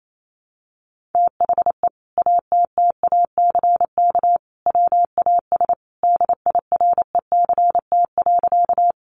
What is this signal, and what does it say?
Morse code, heard as a keyed tone, about 28 wpm, 700 Hz. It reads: THE ATTACK WAS DIRECT.
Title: Sentences from Top 1500 Words 28wpm